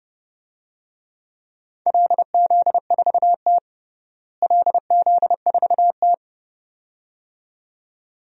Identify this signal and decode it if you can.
Morse code, heard as a keyed tone, about 30 wpm, 700 Hz. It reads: LZ4T LZ4T
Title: DXCC-100 Call Signs - 4 Characters - Repeated 2x - 30wpm